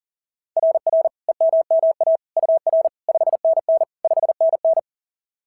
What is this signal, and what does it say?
Morse code, heard as a keyed tone, about 40 wpm, 690 Hz. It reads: RR EMMA UR 5NN 5NN
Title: SOTA QSO Practice 40wpm